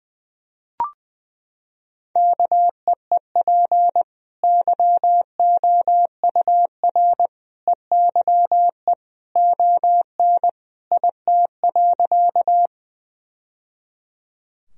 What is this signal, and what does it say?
Morse code, heard as a keyed tone, about 20 wpm, 700 Hz. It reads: KEEP YOUR EYE ON IT.
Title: Sentences from Top 1000 Words - Review Part 1 - 20wpm